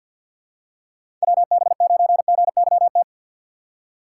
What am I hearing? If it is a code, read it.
Morse code, heard as a keyed tone, about 50 wpm, 700 Hz. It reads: WB9GYT
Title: USA Call Signs - 4x Character Spacing - 50wpm